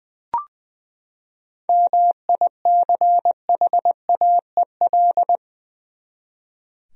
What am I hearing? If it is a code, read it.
Morse code, heard as a keyed tone, about 20 wpm, 700 Hz. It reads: MICHAEL
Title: USA Common Names - 20wpm